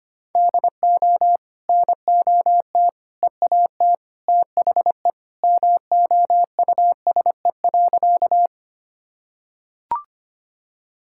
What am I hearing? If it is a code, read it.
Morse code, heard as a keyed tone, about 25 wpm, 700 Hz. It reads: DO NOT EAT THE MOUSE.
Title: Sentences from Top 1000 Words - Review Part 5 - 25wpm